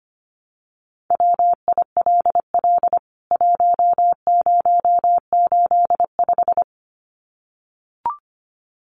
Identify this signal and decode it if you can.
Morse code, heard as a keyed tone, about 25 wpm, 700 Hz. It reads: WILL 1085
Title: CWT Exchange 25wpm